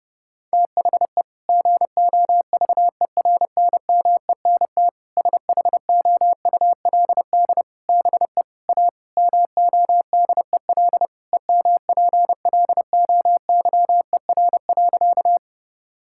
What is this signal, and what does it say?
Morse code, heard as a keyed tone, about 30 wpm, 700 Hz. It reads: THE GOVERNMENT SHOULD BE A MODEL EMPLOYER.
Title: Sentences from Top 1600 Words 30wpm